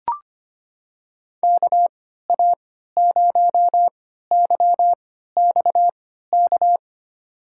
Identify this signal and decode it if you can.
Morse code, heard as a keyed tone, about 25 wpm, spaced as for 14 wpm, 700 Hz. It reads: KA0YXK
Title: USA Call Signs - 3x Character Spacing - 25wpm